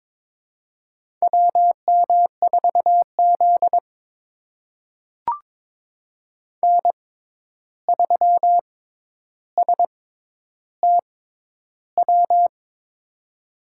Morse code, heard as a keyed tone, about 21 wpm, spaced as for 10 wpm, 700 Hz. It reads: WM4Z N3STW
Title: USA Call Signs - 6x Character Spacing - 22wpm